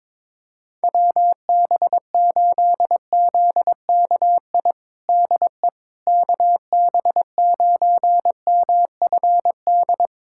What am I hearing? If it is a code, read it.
Morse code, heard as a keyed tone, about 22 wpm, 700 Hz. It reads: WB8ZKI DE KB9MFD
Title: USA Callsign DE Callsign - 6 Characters - 22wpm